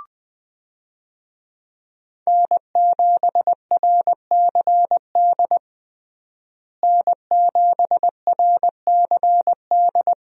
Morse code, heard as a keyed tone, about 21 wpm, 700 Hz. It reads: N7RCD N7RCD
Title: USA Call Signs - 5 Characters - Repeated 3x - 20wpm